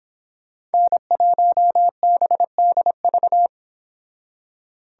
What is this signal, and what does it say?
Morse code, heard as a keyed tone, about 28 wpm, 700 Hz. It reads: N1BDV